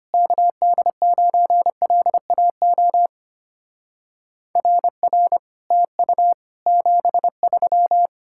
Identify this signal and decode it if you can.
Morse code, heard as a keyed tone, about 33 wpm, 700 Hz. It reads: KD9LAO RR TU 73